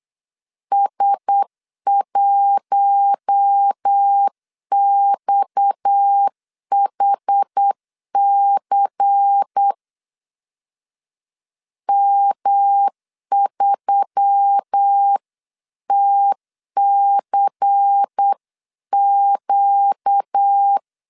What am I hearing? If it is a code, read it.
Morse code, heard as a keyed tone, about 8 wpm, 800 Hz. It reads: S1XHC M3TCQ
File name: 8wpm